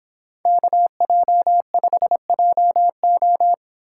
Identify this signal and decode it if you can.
Morse code, heard as a keyed tone, about 26 wpm, 700 Hz. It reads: KJ5JO